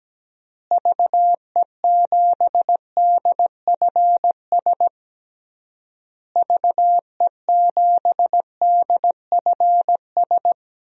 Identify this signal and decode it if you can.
Morse code, heard as a keyed tone, about 17 wpm, 700 Hz. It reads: VE7DFS VE7DFS
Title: DXCC-100 Call Signs - 6 Characters - Repeated 3x - 17wpm